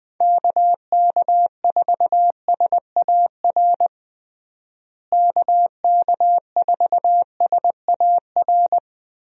Morse code, heard as a keyed tone, about 20 wpm, 700 Hz. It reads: KK4SAR KK4SAR
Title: USA Call Signs - 6 Characters - Repeated 3x - 20wpm